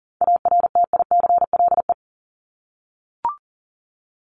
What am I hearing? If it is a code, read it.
Morse code, heard as a keyed tone, about 40 wpm, 700 Hz. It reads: ARTICLE
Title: Top 751-1000 Words 40wpm